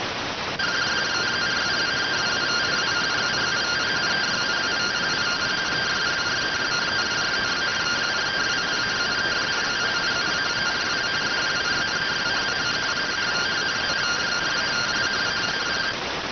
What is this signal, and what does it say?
Two-tone teleprinter signal, about 170 Hz shift.